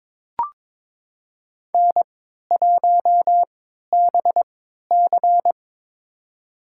Morse code, heard as a keyed tone, about 22 wpm, spaced as for 13 wpm, 700 Hz. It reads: N1BC
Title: USA Call Signs - 4 Characters - 3x Character Spacing - 22wpm